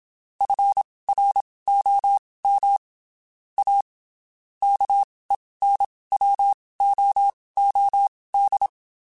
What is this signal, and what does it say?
Morse code, heard as a keyed tone, about 27 wpm, spaced as for 19 wpm, 800 Hz. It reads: FROM A KENWOOD
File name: (spaced) / QSOs Spaced-mp3s / kc7sac @25 sp